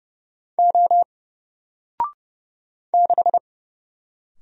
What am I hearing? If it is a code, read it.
Morse code, heard as a keyed tone, about 30 wpm, 700 Hz. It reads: O 6
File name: Single-Letter-Number-30wpm